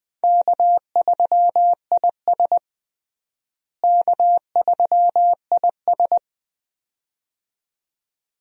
Morse code, heard as a keyed tone, about 20 wpm, 700 Hz. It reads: K3IS K3IS
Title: USA Call Signs - 4 Characters - Repeated 2x - 20wpm